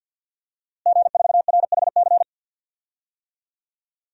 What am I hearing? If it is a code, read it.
Morse code, heard as a keyed tone, about 50 wpm, 700 Hz. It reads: G4RHC
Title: DXCC-100 Call Signs - 5 Characters - Repeated 3x - 50wpm